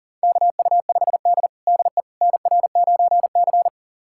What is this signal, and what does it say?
Morse code, heard as a keyed tone, about 37 wpm, 700 Hz. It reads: KU5D DE NR9C
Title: USA Callsign DE Callsign - 4 Characters - 40wpm